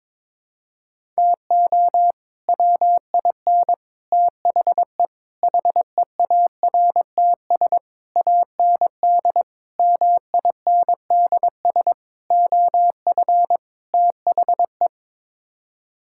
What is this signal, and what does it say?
Morse code, heard as a keyed tone, about 22 wpm, 700 Hz. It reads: TO WIN THE HEARTS AND MINDS OF THE
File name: practice / 22wpm / Sets of 8 Words - 22wpm